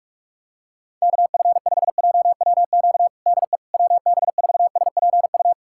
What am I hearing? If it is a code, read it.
Morse code, heard as a keyed tone, about 46 wpm, 700 Hz. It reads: KF5JWQ DE WB4SPU
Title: USA Callsign DE Callsign - 6 Characters - 45wpm